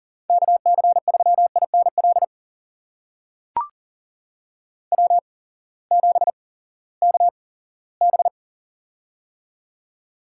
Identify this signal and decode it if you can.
Morse code, heard as a keyed tone, about 42 wpm, spaced as for 20 wpm, 700 Hz. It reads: KC3INL W7KB
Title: USA Call Signs - 8x Character Spacing - 40wpm